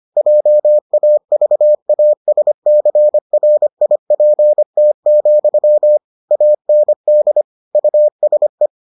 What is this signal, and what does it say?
Morse code, heard as a keyed tone, about 25 wpm, 600 Hz. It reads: JAVASCRIPT, AND USE